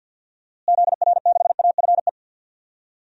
Morse code, heard as a keyed tone, about 46 wpm, 700 Hz. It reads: ZR6AFE
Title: DXCC-100 Call Signs - 6 Characters - Repeated 2x - 50wpm